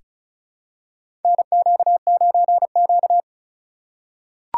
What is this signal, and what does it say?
Morse code, heard as a keyed tone, about 36 wpm, 700 Hz. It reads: NQ9Q
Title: USA Call Signs - 4 Characters - 2x Character Spacing - 35wpm